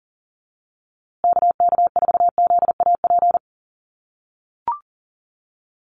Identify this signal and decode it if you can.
Morse code, heard as a keyed tone, about 40 wpm, 700 Hz. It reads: KK4ZAP
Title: USA Call Signs Repeated 2x 40wpm